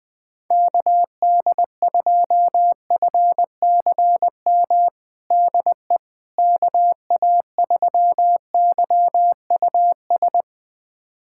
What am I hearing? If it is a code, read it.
Morse code, heard as a keyed tone, about 21 wpm, 700 Hz. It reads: KD2FCM DE KA3YUS
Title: USA Callsign DE Callsign - 6 Characters - 20wpm